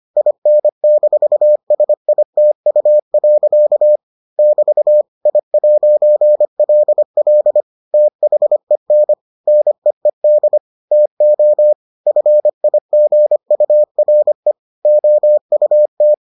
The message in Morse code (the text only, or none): IN-SITU. <BT> I'LL THEN NEED TO FIGURE OUT